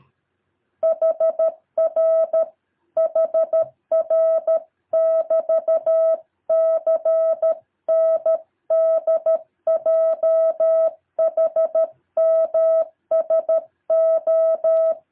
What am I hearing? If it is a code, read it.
Morse code, heard as a keyed tone, about 13 wpm, 650 Hz. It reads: HR HR <BT> CNDJHMSO
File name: jh8cbh / 2023cw